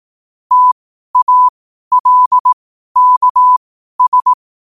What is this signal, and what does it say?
Morse code, heard as a keyed tone, about 18 wpm, spaced as for 12 wpm, 1000 Hz. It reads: TALKS
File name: qotdfiles / 12wpm